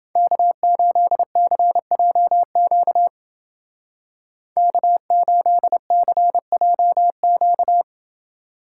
Morse code, heard as a keyed tone, about 28 wpm, 700 Hz. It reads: K8CJQ K8CJQ